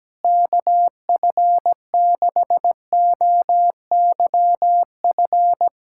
Morse code, heard as a keyed tone, about 17 wpm, 700 Hz. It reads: KF6OYF
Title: USA Callsign DE Callsign - 6 Characters - 17wpm